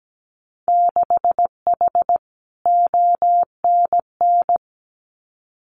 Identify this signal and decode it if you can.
Morse code, heard as a keyed tone, about 17 wpm, 700 Hz. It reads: 6H ONN
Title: Winter Field Day Exchanges Encore 17wpm